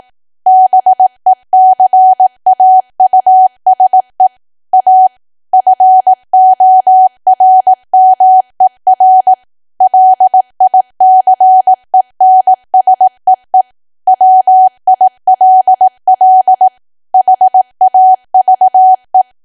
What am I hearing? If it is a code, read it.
Morse code, heard as a keyed tone, about 18 wpm, 750 Hz. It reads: BECAUSE A FORMER LICENSEE WILL HAVE
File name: cwa / W1AW-Level2 / 18WPM